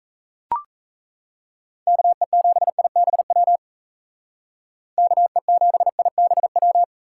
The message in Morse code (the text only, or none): KE7IBW KE7IBW